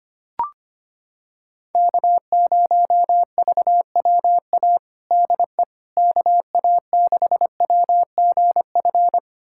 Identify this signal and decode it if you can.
Morse code, heard as a keyed tone, about 25 wpm, 700 Hz. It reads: K0VWA DE KA6WGF
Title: USA Callsign DE Callsign 25wpm